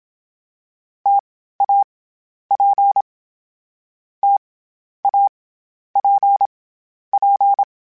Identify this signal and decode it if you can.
Morse code, heard as a keyed tone, about 27 wpm, 800 Hz. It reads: T A P T A P P
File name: Difficult Combos / Difficult Combos-mp3s / T AOEI @25